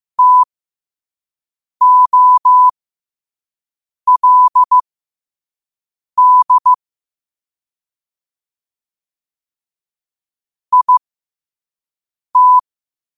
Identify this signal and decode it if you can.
Morse code, heard as a keyed tone, about 15 wpm, spaced as for 5 wpm, 1000 Hz. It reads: TOLD IT
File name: qotdfiles / 5wpm